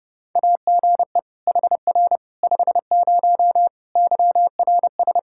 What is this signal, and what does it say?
Morse code, heard as a keyed tone, about 30 wpm, 700 Hz. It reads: AGE HR 50 YRS
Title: K1USN SST QSO Practice 30wpm